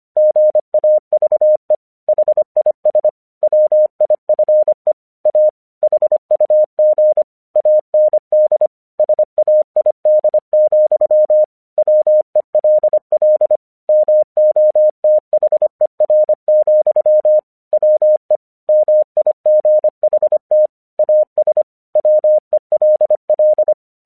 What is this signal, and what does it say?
Morse code, heard as a keyed tone, about 25 wpm, 610 Hz. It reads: GAVE HIS WIFE A HUG AND SAID, WELL MOTHER, WE MIGHT AS WELL
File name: 1 English 25wpm / AQC05